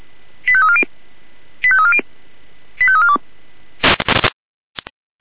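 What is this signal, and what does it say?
Five-tone selective call (ZVEI), heard as quick runs of five tones, tones of about 70 ms. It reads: 85340 85340 85342
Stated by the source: Signal received via SDR, ZVEI five-tone.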